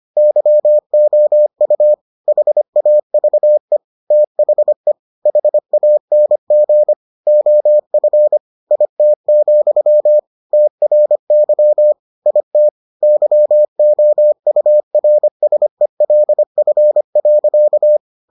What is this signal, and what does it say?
Morse code, heard as a keyed tone, about 25 wpm, 600 Hz. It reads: YOU HAVE THE HANG OF IT, TRY IT YOURSELF.